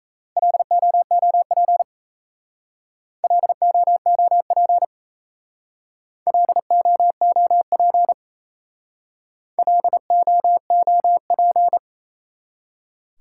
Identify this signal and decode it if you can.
Morse code, heard as a keyed tone, about 36 wpm, 700 Hz. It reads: LOOP LOOP LOOP LOOP